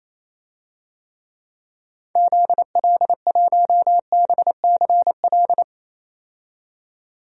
Morse code, heard as a keyed tone, about 28 wpm, 700 Hz. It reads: ZL1BCL
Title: DXCC-100 Call Signs - 6 Characters - 28wpm